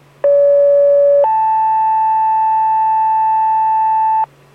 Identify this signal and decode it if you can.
Two-tone page, heard as a one-second tone, then a three-second tone, about 570 then 880 Hz, with noise. It is heard from 0.2 to 4.2 s.